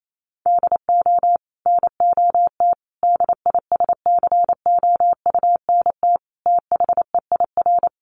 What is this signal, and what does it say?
Morse code, heard as a keyed tone, about 28 wpm, 700 Hz. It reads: DO NOT DISCOUNT THEIR
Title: Sentences from Top 1200 Words 28wpm